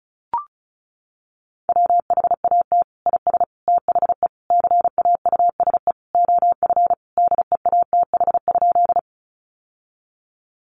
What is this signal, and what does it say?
Morse code, heard as a keyed tone, about 34 wpm, 700 Hz. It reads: WHAT IS THE CAUSE OF DEATH?
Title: Sentences from Top 1100 Words 35wpm